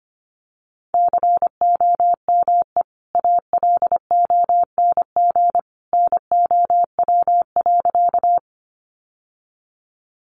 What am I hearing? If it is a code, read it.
Morse code, heard as a keyed tone, about 25 wpm, 700 Hz. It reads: COME ALONG NOW.
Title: Sentences from Top 700 Words 25wpm